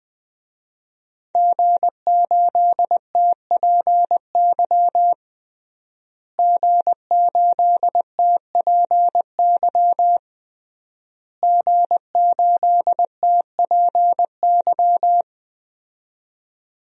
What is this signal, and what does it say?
Morse code, heard as a keyed tone, about 20 wpm, 700 Hz. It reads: G8TPY G8TPY G8TPY
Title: DXCC-100 Call Signs - 5 Characters - Repeated 3x - 20wpm